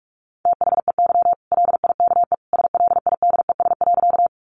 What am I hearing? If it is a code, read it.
Morse code, heard as a keyed tone, about 44 wpm, 700 Hz. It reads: THEY LIKE SLIDES.
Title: Easy Phrases 45wpm